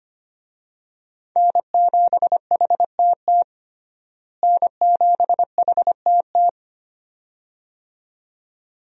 Morse code, heard as a keyed tone, about 25 wpm, 700 Hz. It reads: N7HTT N7HTT
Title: USA Call Signs - 5 Characters - Repeated 2x - 25wpm